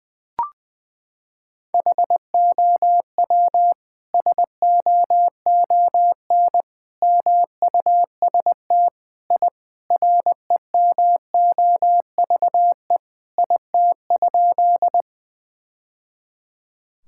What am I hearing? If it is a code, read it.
Morse code, heard as a keyed tone, about 20 wpm, 700 Hz. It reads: HOW SOON MUST I REMOVE IT?
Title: Sentences from Top 1000 Words - Review Part 5 - 20wpm